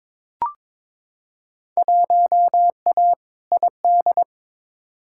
Morse code, heard as a keyed tone, about 22 wpm, 700 Hz. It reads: 1A ID
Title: Field Day Exchanges - 22wpm